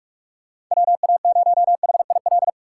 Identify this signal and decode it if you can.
Morse code, heard as a keyed tone, about 43 wpm, 700 Hz. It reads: WA0HIL